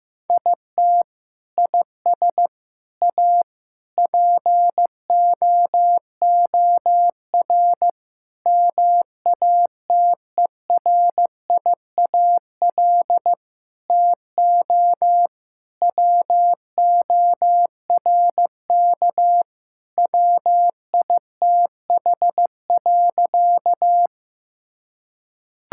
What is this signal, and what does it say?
Morse code, heard as a keyed tone, about 15 wpm, 700 Hz. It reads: IT IS A POOR MATERIAL TO WORK WITH.
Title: Sentences from Top 1000 Words - Review Part 5 - 15wpm